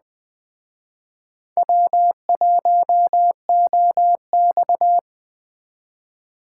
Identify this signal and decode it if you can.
Morse code, heard as a keyed tone, about 20 wpm, 700 Hz. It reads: W1OX